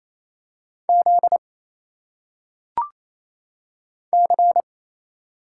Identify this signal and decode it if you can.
Morse code, heard as a keyed tone, about 27 wpm, 700 Hz. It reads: Z C
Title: Single Letters - Phonetic Pronunciation 28wpm